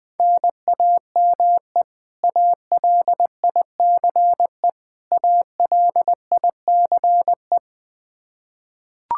Morse code, heard as a keyed tone, about 20 wpm, 700 Hz. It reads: NAME ALICE ALICE
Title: Practice QSO - Exchange 1 - Beginning - 20wpm